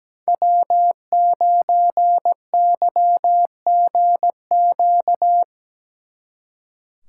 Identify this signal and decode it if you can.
Morse code, heard as a keyed tone, about 19 wpm, 700 Hz. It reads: W9YGQ
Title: Speed Racing - USA 1x3 Call Signs 17wpm